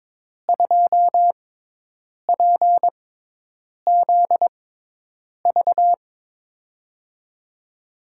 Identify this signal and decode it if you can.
Morse code, heard as a keyed tone, about 23 wpm, spaced as for 8 wpm, 700 Hz. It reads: 2PZV
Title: USA Call Signs - 6x Character Spacing - 22wpm